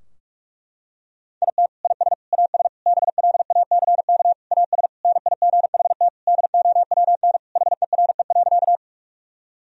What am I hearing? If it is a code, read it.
Morse code, heard as a keyed tone, about 46 wpm, 700 Hz. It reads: IT IS AS BLACK AS NIGHT DOWN HERE.
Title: Sentences from Top 1000 Words - Review Part 2 - 45wpm